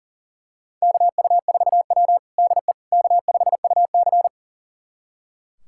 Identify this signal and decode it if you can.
Morse code, heard as a keyed tone, about 40 wpm, 700 Hz. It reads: KU4W DE K5UC
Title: USA Callsign DE Callsign - 4 Characters - 40wpm